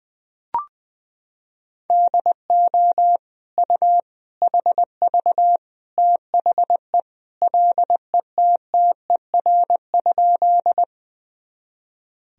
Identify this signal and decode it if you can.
Morse code, heard as a keyed tone, about 20 wpm, 700 Hz. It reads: DO U HV THE LETTER?
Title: Sentences from Top 300 Words and Abbreviations 20wpm